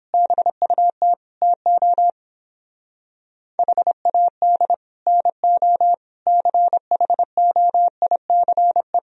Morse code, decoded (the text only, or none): BUT TO HAD NO CHOICE